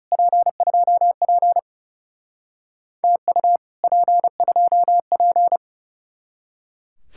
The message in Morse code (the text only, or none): P2P TU P2P